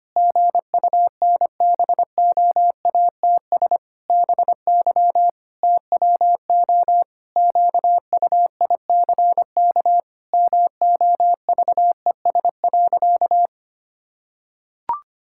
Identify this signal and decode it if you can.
Morse code, heard as a keyed tone, about 25 wpm, 700 Hz. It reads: GUNBOATS BY TWO QUICK MOVES.
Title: Pangram Challenge v2 - 25wpm